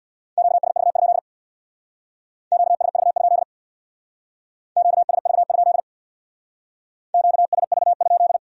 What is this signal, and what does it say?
Morse code, heard as a keyed tone, about 64 wpm, 700 Hz. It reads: QSV? QSV? QSV? QSV?